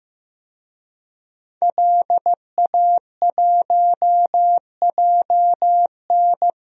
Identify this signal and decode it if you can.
Morse code, heard as a keyed tone, about 15 wpm, 700 Hz. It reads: LA1JN
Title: DXCC-100 Call Signs - 5 Characters - 15wpm